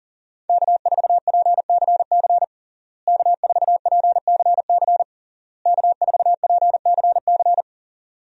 Morse code, heard as a keyed tone, about 40 wpm, 700 Hz. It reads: K4PCC K4PCC K4PCC
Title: USA Call Signs - 5 Characters - Repeated 3x - 40wpm